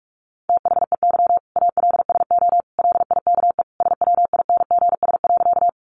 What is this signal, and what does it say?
Morse code, heard as a keyed tone, about 44 wpm, 700 Hz. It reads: THEY ALSO LIKE SWINGS.
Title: Easy Phrases 45wpm